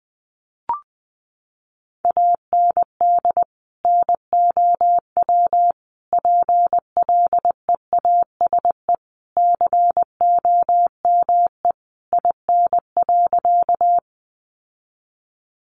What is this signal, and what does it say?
Morse code, heard as a keyed tone, about 20 wpm, 700 Hz. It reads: AND NOW PLEASE COME IN.
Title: Sentences from Top 1500 Words 20wpm